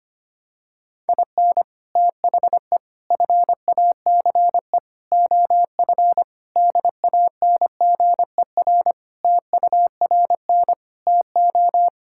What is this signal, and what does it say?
Morse code, heard as a keyed tone, about 25 wpm, 700 Hz. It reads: IN THE FACE OF DANGER TURN TO
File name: practice / 25wpm / Sentences with 3-Word Phrases - 25wpm